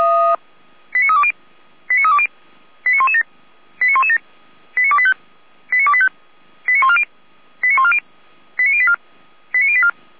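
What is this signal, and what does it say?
Five-tone selective call (ZVEI), heard as quick runs of five tones, tones of about 70 ms. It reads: 89320 89320 89117 89117 89285 89285 89140 89140 89084 89084
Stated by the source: ZVEI five-tone selective call received by SDR.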